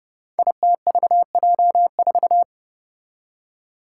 Morse code, heard as a keyed tone, about 30 wpm, 700 Hz. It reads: ITVJ4
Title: Five Challenging Letter-Number Combinations 30wpm